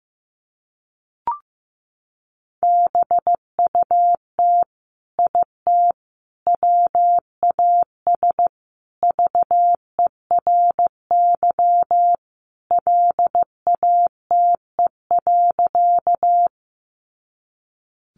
Morse code, heard as a keyed tone, about 15 wpm, 700 Hz. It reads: BUT IT WAS VERY LATE.
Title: Sentences from Top 300 Words 15wpm